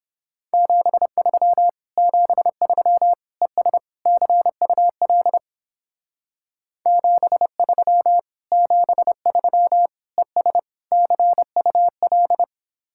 Morse code, heard as a keyed tone, about 31 wpm, 700 Hz. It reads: 73 73 ES CUL 73 73 ES CUL